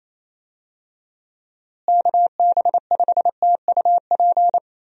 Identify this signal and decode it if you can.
Morse code, heard as a keyed tone, about 27 wpm, 700 Hz. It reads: KB5TUP
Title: USA Call Signs - 6 Characters - Repeated 2x - 28wpm